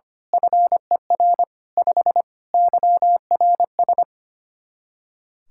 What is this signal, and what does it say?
Morse code, heard as a keyed tone, about 25 wpm, 700 Hz. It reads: FER 5 YRS
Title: Practice QSO - Exchange 3 - More Info - 25wpm